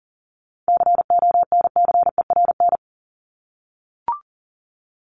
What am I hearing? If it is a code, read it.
Morse code, heard as a keyed tone, about 40 wpm, 700 Hz. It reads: CONCERN
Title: Top 751-1000 Words 40wpm